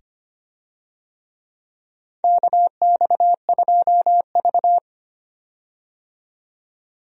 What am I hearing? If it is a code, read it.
Morse code, heard as a keyed tone, about 25 wpm, 700 Hz. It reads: KX2V
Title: USA Call Signs - 4 Characters - 25wpm